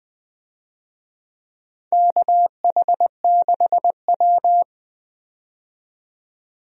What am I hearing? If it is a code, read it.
Morse code, heard as a keyed tone, about 20 wpm, 700 Hz. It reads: KH6W
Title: DXCC-100 Call Signs - 4 Characters - 20wpm